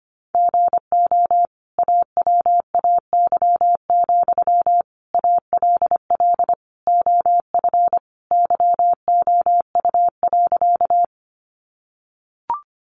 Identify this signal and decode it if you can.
Morse code, heard as a keyed tone, about 25 wpm, 700 Hz. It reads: GO AWAY, ALL OF YOU.
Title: Sentences from Top 600 Words 25wpm